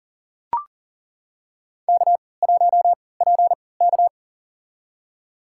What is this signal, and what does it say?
Morse code, heard as a keyed tone, about 38 wpm, spaced as for 22 wpm, 700 Hz. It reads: K1PK
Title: USA Call Signs - 3x Character Spacing - 40wpm